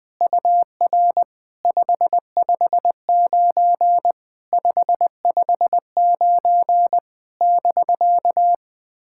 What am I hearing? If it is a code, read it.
Morse code, heard as a keyed tone, about 20 wpm, 700 Hz. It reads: UR 559 559 <BK>
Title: SOTA QSO Practice 20wpm